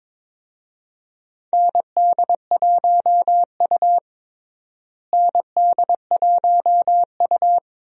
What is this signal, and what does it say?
Morse code, heard as a keyed tone, about 22 wpm, 700 Hz. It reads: ND1U ND1U
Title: USA Call Signs - 4 Characters - Repeated 2x - 22wpm